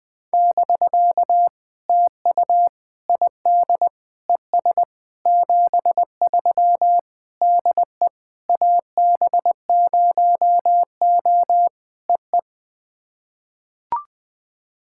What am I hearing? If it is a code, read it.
Morse code, heard as a keyed tone, about 21 wpm, 700 Hz. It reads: <BK> TU ID ES 73 DE AB0O EE
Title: POTA QSO Practice - 20wpm